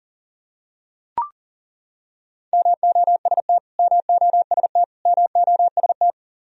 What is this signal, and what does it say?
Morse code, heard as a keyed tone, about 38 wpm, 700 Hz. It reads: MOST MOST MOST
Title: Top 100 Words Repeated 40wpm